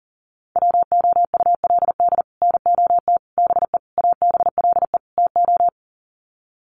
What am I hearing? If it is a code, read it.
Morse code, heard as a keyed tone, about 40 wpm, 700 Hz. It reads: WOULD NOT BE ABLE TO
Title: Sets of 5 Words 40wpm